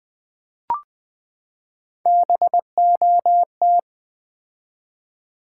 Morse code, heard as a keyed tone, about 20 wpm, 700 Hz. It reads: BOT
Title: Build A Word - 3 Letters At A Time - 6-Letter Words - 20wpm